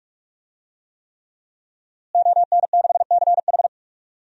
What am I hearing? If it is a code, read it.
Morse code, heard as a keyed tone, about 46 wpm, 700 Hz. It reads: ON6CH
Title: DXCC-100 Call Signs - 5 Characters - Repeated 3x - 45wpm